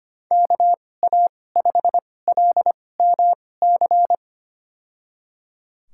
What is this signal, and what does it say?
Morse code, heard as a keyed tone, about 25 wpm, spaced as for 18 wpm, 700 Hz. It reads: KA5LMC
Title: USA Call Signs - 6 Characters - 2x Character Spacing - 25wpm